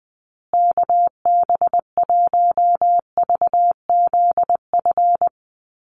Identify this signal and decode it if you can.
Morse code, heard as a keyed tone, about 20 wpm, 700 Hz. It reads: KB1VZF
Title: USA Call Signs Repeated 2x 20wpm